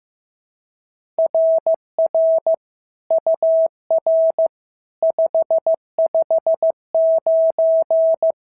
Morse code, heard as a keyed tone, about 14 wpm, 650 Hz. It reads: RR UR 559
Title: SOTA QSO Practice 15wpm